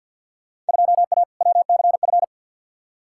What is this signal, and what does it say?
Morse code, heard as a keyed tone, about 52 wpm, 700 Hz. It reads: 2A WCF
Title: Field Day Exchanges Encore - 50wpm